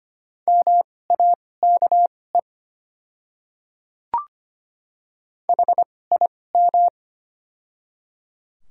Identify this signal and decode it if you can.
Morse code, heard as a keyed tone, about 25 wpm, spaced as for 18 wpm, 700 Hz. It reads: MAKE HIM
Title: 2x Character Spacing - Top 100 Words - 25wpm